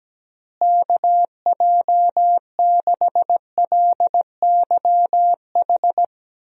Morse code, heard as a keyed tone, about 17 wpm, 700 Hz. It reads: KJ6LYH
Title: USA Call Signs - 6 Characters - Repeated 2x - 17wpm